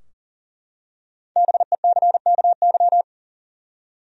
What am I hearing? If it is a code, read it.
Morse code, heard as a keyed tone, about 40 wpm, 700 Hz. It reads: BECKY
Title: USA Common Names - Max 5 Letters - 40wpm